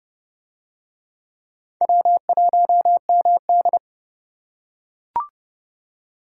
Morse code, heard as a keyed tone, about 29 wpm, 700 Hz. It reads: W1MD